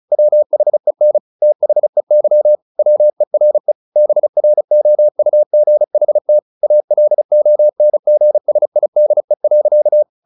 Morse code, heard as a keyed tone, about 34 wpm, 600 Hz. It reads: WHEN THEY WERE BROUGHT ALONGSIDE.
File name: the old man and the sea / the old man and the sea 1 35wpm